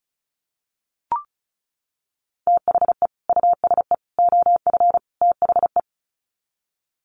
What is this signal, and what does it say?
Morse code, heard as a keyed tone, about 34 wpm, 700 Hz. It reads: THE USE OF THE
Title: Sets of 4 Words 35wpm